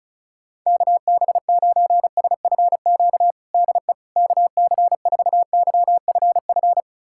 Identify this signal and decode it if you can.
Morse code, heard as a keyed tone, about 36 wpm, 700 Hz. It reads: KB9SFQ DE KC4YFF
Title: USA Callsign DE Callsign - 6 Characters - 35wpm